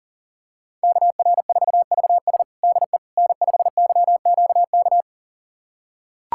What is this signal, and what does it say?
Morse code, heard as a keyed tone, about 40 wpm, 700 Hz. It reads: KR4VS DE N5YQK
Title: USA Callsign DE Callsign - 5 Characters - 40wpm